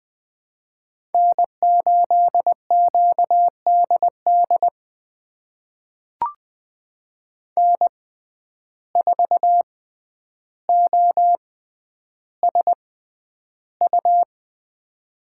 Morse code, heard as a keyed tone, about 20 wpm, spaced as for 10 wpm, 700 Hz. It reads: N8QDD N4OSU
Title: USA Call Signs - 5 Characters - 6x Character Spacing - 20wpm